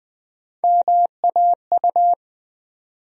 Morse code, heard as a keyed tone, about 20 wpm, 700 Hz. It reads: MAU
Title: USA Call Sign Postfixes - 3 Letters 20wpm